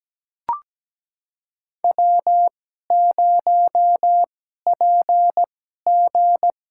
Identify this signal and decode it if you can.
Morse code, heard as a keyed tone, about 17 wpm, spaced as for 12 wpm, 700 Hz. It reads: W0PG
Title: USA Call Signs - 4 Characters - 2x Character Spacing - 17wpm